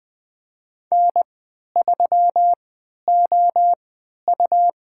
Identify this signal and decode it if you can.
Morse code, heard as a keyed tone, about 20 wpm, spaced as for 11 wpm, 700 Hz. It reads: N3OU
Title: USA Call Signs - 4 Characters - 3x Character Spacing - 20wpm